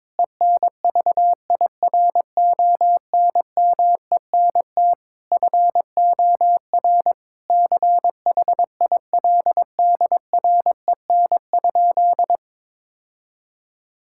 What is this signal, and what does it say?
Morse code, heard as a keyed tone, about 22 wpm, 700 Hz. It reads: ENVIRONMENT FOR CHILDREN?
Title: Sentences from Top 1000 Words - Review Part 5 - 22wpm